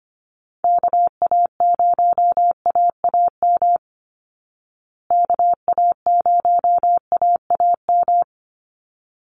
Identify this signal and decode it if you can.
Morse code, heard as a keyed tone, about 25 wpm, 700 Hz. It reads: KA0AAM KA0AAM
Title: Call Signs Repeated 3x - 25wpm